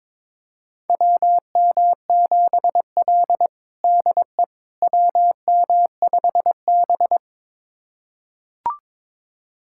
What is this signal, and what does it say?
Morse code, heard as a keyed tone, about 22 wpm, 700 Hz. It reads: WM7L DE WM5B
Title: USA Callsign DE Callsign - 4 Characters - 22wpm